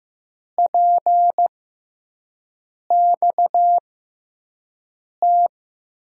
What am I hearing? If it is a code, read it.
Morse code, heard as a keyed tone, about 15 wpm, spaced as for 5 wpm, 700 Hz. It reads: PXT
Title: USA Call Signs - 6 Characters - 6x Character Spacing - 15wpm